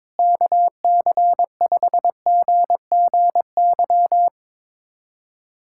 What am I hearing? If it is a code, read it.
Morse code, heard as a keyed tone, about 22 wpm, 700 Hz. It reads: KC5GGY
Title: USA Callsign DE Callsign - 6 Characters - 22wpm